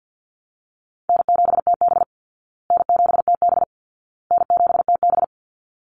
Morse code, heard as a keyed tone, about 50 wpm, 700 Hz. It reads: N7TB N7TB N7TB
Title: Easy Call Signs Repeated 3x - 50wpm